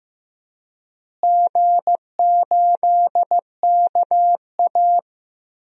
Morse code, heard as a keyed tone, about 15 wpm, 700 Hz. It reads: G8KA
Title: DXCC-100 Call Signs - 4 Characters - Repeated 3x - 15wpm